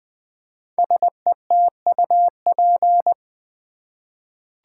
Morse code, heard as a keyed tone, about 20 wpm, 700 Hz. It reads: SETUP
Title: 5 Letter Words - First 3 Letters Given - 20wpm